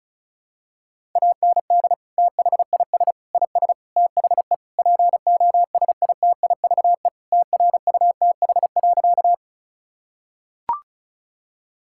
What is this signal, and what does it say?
Morse code, heard as a keyed tone, about 36 wpm, 700 Hz. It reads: AND THIS IS THE POSITIVE TRUTH.
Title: Sentences from Top 1000 Words - Review Part 5 - 35wpm